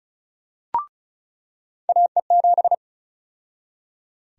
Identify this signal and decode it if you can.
Morse code, heard as a keyed tone, about 36 wpm, 700 Hz. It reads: AE7
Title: USA Call Sign Prefixes - 35wpm